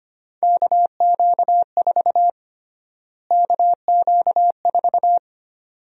Morse code, heard as a keyed tone, about 25 wpm, 700 Hz. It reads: KQ4 KQ4
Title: USA Call Sign Prefixes - Repeated 2x - 25wpm